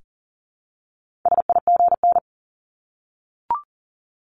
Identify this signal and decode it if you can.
Morse code, heard as a keyed tone, about 40 wpm, 700 Hz. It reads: SIGN
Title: Multi-letter Phonemes Repeated 3x 40wpm